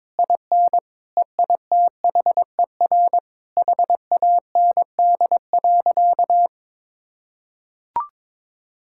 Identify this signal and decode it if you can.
Morse code, heard as a keyed tone, about 21 wpm, 700 Hz. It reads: IN EITHER HAND.
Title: Sentences from Top 1600 Words 22wpm